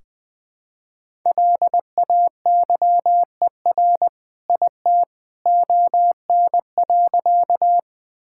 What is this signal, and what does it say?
Morse code, heard as a keyed tone, about 20 wpm, 700 Hz. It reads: LAYER IT ON.
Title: Sentences from Top 1600 Words 20wpm